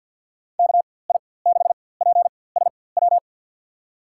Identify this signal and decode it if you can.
Morse code, heard as a keyed tone, about 47 wpm, spaced as for 23 wpm, 700 Hz. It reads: KI6PSW
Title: USA Call Signs - 6 Characters - 4x Character Spacing - 50wpm